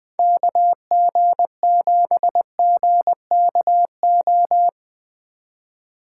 Morse code, heard as a keyed tone, about 20 wpm, 700 Hz. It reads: KG7GKO